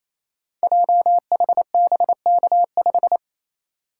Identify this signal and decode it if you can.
Morse code, heard as a keyed tone, about 29 wpm, 700 Hz. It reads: JHBK5
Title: Code Groups - 5 Letter:Numbers - 28wpm